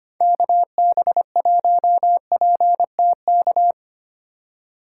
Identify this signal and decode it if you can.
Morse code, heard as a keyed tone, about 30 wpm, 700 Hz. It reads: KB1PTK
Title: Speed Racing - USA 2x3 Call Signs 25wpm